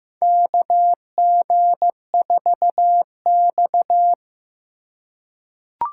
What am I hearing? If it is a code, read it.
Morse code, heard as a keyed tone, about 15 wpm, 700 Hz. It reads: KG4X
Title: USA Call Signs - 4 Characters - 2x Character Spacing - 15wpm